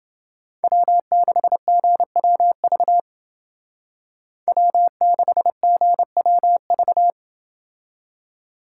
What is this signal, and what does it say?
Morse code, heard as a keyed tone, about 28 wpm, 700 Hz. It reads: W6GWV W6GWV